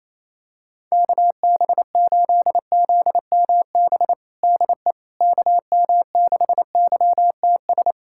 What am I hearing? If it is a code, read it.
Morse code, heard as a keyed tone, about 28 wpm, 700 Hz. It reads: KB8ZMB DE KM6YTS
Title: USA Callsign DE Callsign - 6 Characters - 28wpm